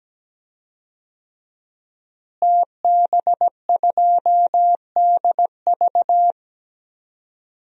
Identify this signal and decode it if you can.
Morse code, heard as a keyed tone, about 17 wpm, 700 Hz. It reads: TB2DV
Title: DXCC-100 Call Signs - 5 Characters - Repeated 2x - 17wpm